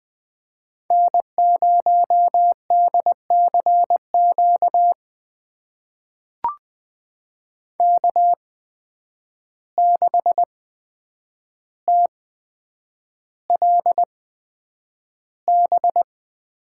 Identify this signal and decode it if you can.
Morse code, heard as a keyed tone, about 20 wpm, spaced as for 9 wpm, 700 Hz. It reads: N0DCQ K6TLB
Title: USA Call Signs - 5 Characters - 8x Character Spacing - 20wpm